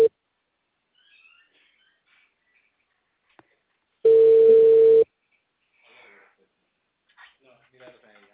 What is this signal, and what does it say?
Ringback tone (China, 450 Hz), with speech.